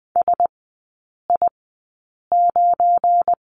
Morse code, heard as a keyed tone, about 20 wpm, spaced as for 8 wpm, 690 Hz. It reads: SI9
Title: 212 Lesson 34e - Number 8 - Three Characters 20wpm